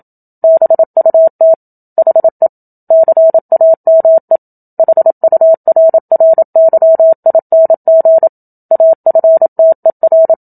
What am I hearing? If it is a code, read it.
Morse code, heard as a keyed tone, about 27 wpm, 650 Hz. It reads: BUT HE CAME HURRYING AFTER
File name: Book Two / 01-Chapter 1 - 27 WPM